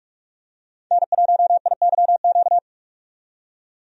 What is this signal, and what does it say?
Morse code, heard as a keyed tone, about 41 wpm, 700 Hz. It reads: N1IYQ
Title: USA Call Signs - 5 Characters - 45wpm